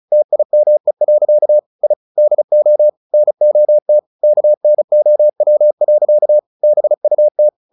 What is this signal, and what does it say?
Morse code, heard as a keyed tone, about 35 wpm, 600 Hz. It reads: TIME. I DO NOT KNOW. BUT
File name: the old man and the sea 2 35wpm